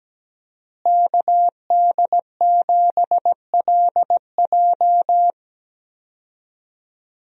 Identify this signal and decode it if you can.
Morse code, heard as a keyed tone, about 17 wpm, 700 Hz. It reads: KD7LJ
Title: USA Call Signs - 5 Characters - Repeated 3x - 17wpm